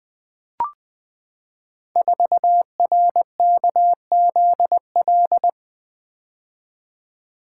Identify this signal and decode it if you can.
Morse code, heard as a keyed tone, about 20 wpm, 700 Hz. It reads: 4RKZL
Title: Code Groups - 5 Letter:Numbers - 20wpm